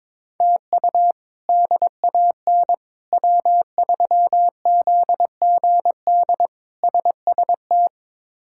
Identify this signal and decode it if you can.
Morse code, heard as a keyed tone, about 22 wpm, 700 Hz. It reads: TU DAN W3ZGD SST
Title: K1USN SST QSO Practice - 22wpm